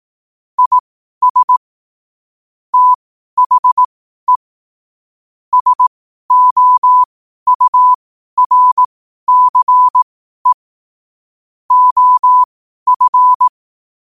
Morse code, heard as a keyed tone, about 18 wpm, spaced as for 12 wpm, 1000 Hz. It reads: IS THE SOURCE OF
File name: qotdfiles / 12wpm